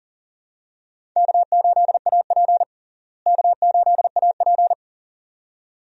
Morse code, heard as a keyed tone, about 40 wpm, 700 Hz. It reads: K8AP K8AP
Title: USA Call Signs - 4 Characters - Repeated 2x - 40wpm